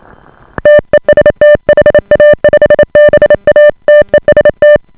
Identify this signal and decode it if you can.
Morse code, heard as a keyed tone, about 28 wpm, 590 Hz. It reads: TEST HA5BA TEST